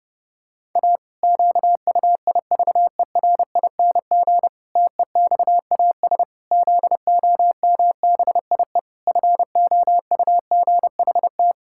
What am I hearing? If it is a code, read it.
Morse code, heard as a keyed tone, about 30 wpm, 700 Hz. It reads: A QUIVERING TEXAS ZOMBIE FOUGHT
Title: Pangram Challenge v2 - 30wpm